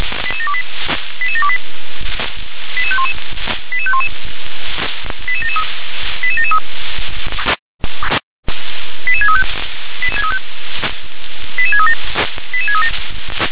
Five-tone selective call (ZVEI), heard as quick runs of five tones, tones of about 70 ms. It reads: 88728 88728 88511 88511 88773 88773 88635 88637 88637
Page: SDR/ZVEI